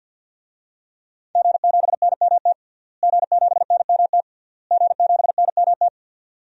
Morse code, heard as a keyed tone, about 52 wpm, 700 Hz. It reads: G7NMT G7NMT G7NMT
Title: DXCC-100 Call Signs - 5 Characters - Repeated 3x - 50wpm